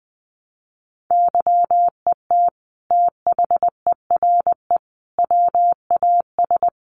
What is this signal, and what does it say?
Morse code, heard as a keyed tone, about 20 wpm, 700 Hz. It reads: YET THERE WAS
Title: Sentences from Top 1300 Words 20wpm